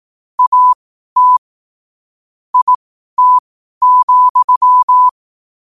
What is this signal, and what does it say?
Morse code, heard as a keyed tone, about 18 wpm, spaced as for 12 wpm, 1000 Hz. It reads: AT IT,
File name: qotdfiles / 12wpm